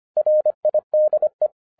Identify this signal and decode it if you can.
Morse code, heard as a keyed tone, about 25 wpm, 610 Hz. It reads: RIDE